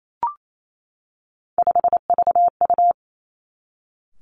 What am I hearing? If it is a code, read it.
Morse code, heard as a keyed tone, about 28 wpm, 700 Hz. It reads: 5VU
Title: Three Challenging Letter-Number Combinations 28wpm